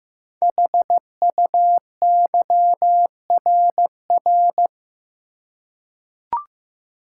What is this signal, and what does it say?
Morse code, heard as a keyed tone, about 15 wpm, 700 Hz. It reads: HUYRR
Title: Code Groups - 5 Letters - 15wpm